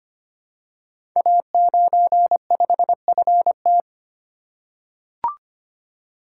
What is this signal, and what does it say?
Morse code, heard as a keyed tone, about 25 wpm, 700 Hz. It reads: A95FT
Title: Code Groups - 5 Letter:Numbers - 25wpm